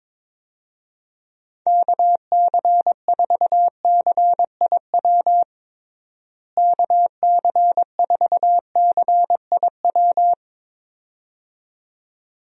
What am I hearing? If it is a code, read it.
Morse code, heard as a keyed tone, about 22 wpm, 700 Hz. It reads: KC4CIW KC4CIW
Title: USA Call Signs - 6 Characters - Repeated 2x - 22wpm